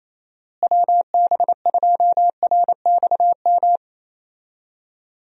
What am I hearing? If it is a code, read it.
Morse code, heard as a keyed tone, about 28 wpm, 700 Hz. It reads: WB2RXM
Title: USA Call Signs - 6 Characters - Repeated 2x - 28wpm